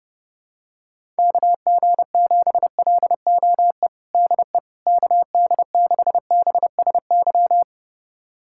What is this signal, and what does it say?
Morse code, heard as a keyed tone, about 30 wpm, 700 Hz. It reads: KG7LOE DE KD6BSY
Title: USA Callsign DE Callsign - 6 Characters - 30wpm